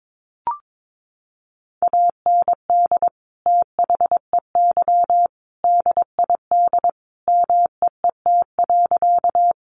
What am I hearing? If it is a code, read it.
Morse code, heard as a keyed tone, about 22 wpm, 700 Hz. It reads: AND THEY DID MEET.
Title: Sentences from Top 1100 Words 22wpm